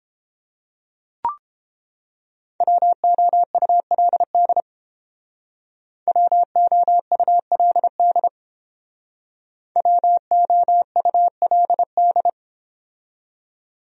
Morse code, heard as a keyed tone, about 31 wpm, 700 Hz. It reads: WOULD WOULD WOULD